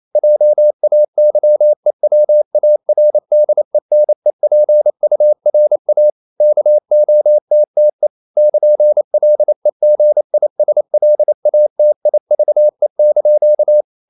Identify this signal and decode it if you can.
Morse code, heard as a keyed tone, about 28 wpm, 600 Hz. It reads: JAYEWARDENEPURA KOTTE <KN>LEGISLATIVE)
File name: CW / Practice / NationalCapitals28